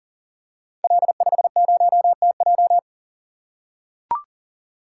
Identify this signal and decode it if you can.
Morse code, heard as a keyed tone, about 42 wpm, 690 Hz. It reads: L50TJ
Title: Code Groups - 5 Letter:Numbers - 40wpm